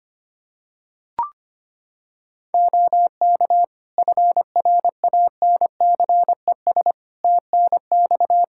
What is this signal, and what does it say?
Morse code, heard as a keyed tone, about 25 wpm, 700 Hz. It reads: OK FRANCES TNX
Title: Practice QSO - Exchange 4 - Ending - 25wpm